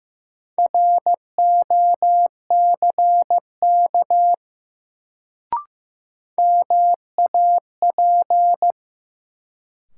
Morse code, heard as a keyed tone, about 15 wpm, 700 Hz. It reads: ROCK MAP
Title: Sentences from Top 400 Words - New Word Drill - 15wpm